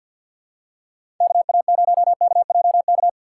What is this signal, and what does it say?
Morse code, heard as a keyed tone, about 53 wpm, 700 Hz. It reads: KA0KJK
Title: USA Call Signs - 6 Characters - 6x Character Spacing - 50wpm